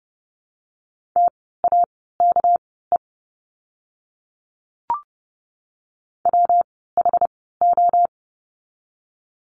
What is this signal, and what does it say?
Morse code, heard as a keyed tone, about 30 wpm, spaced as for 17 wpm, 700 Hz. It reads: TAKE WHO
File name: course / 30:10wpm / 237 Lesson 38f - Letter q - Words 30wpm